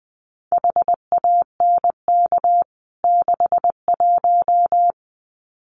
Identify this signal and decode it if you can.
Morse code, heard as a keyed tone, about 20 wpm, 700 Hz. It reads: HANK 61
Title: CWT Exchange 20wpm